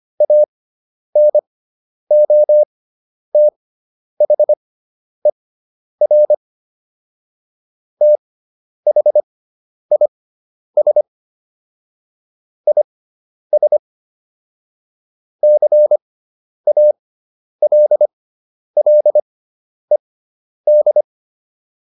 Morse code, heard as a keyed tone, about 25 wpm, spaced as for 10 wpm, 600 Hz. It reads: ANOTHER THIS IS CALLED